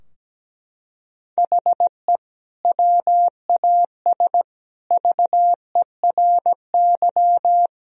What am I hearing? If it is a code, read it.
Morse code, heard as a keyed tone, about 17 wpm, 700 Hz. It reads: HE WAS VERY
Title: Sentences from Top 1000 Words - Review Part 3 - 17wpm